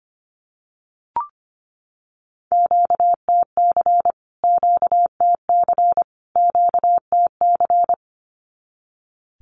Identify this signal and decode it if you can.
Morse code, heard as a keyed tone, about 25 wpm, 700 Hz. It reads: QTC QTC QTC
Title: Q-Codes Repeated 3x 25wpm